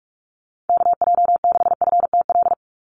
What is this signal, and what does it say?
Morse code, heard as a keyed tone, about 46 wpm, 690 Hz. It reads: KJ6FTL
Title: USA Call Signs 45wpm